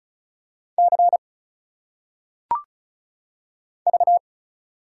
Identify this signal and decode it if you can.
Morse code, heard as a keyed tone, about 35 wpm, 700 Hz. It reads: C V
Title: Single Letters - Phonetic Pronunciation 35wpm